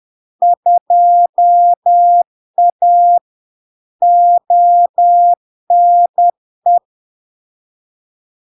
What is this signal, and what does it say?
Morse code, heard as a keyed tone, about 10 wpm, 700 Hz. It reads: 2A ONE
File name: LARK-FD-2022-Log-in-CW-10wpm-700-Hz